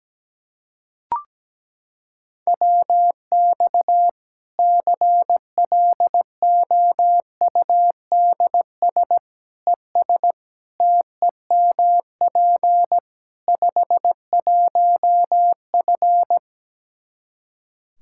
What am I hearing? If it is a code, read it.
Morse code, heard as a keyed tone, about 17 wpm, 700 Hz. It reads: WX CLOUDS ES TEMP 51F
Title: Practice QSO - Exchange 2 - Info - 17wpm